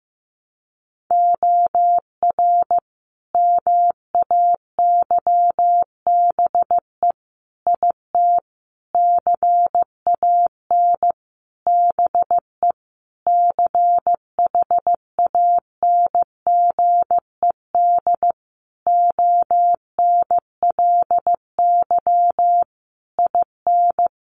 Morse code, heard as a keyed tone, about 15 wpm, 700 Hz. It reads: OR MAYBE IT CAN BE CHANGED ONLY IN
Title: A Sound Of Thunder 15wpm